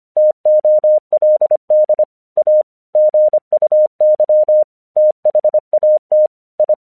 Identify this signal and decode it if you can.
Morse code, heard as a keyed tone, about 25 wpm, 610 Hz. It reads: TOLD A GUY THAT I
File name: 1 English 25wpm / AQC36